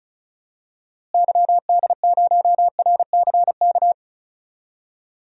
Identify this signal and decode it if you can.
Morse code, heard as a keyed tone, about 36 wpm, 700 Hz. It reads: YD0RCK
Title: DXCC-100 Call Signs - 6 Characters - Repeated 3x - 35wpm